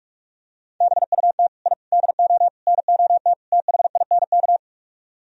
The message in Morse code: BUT I DO NOT THINK

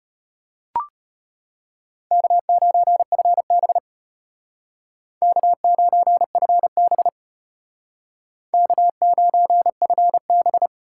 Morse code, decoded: K9FB K9FB K9FB

